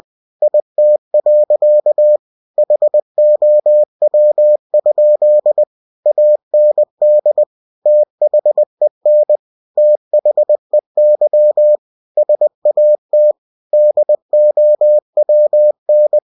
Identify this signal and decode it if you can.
Morse code, heard as a keyed tone, about 20 wpm, 600 Hz. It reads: IT. HOW? AND THEN THEY SAT DOWN
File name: cw-books / winnie-the-pooh / winnie-20wpm-page-0034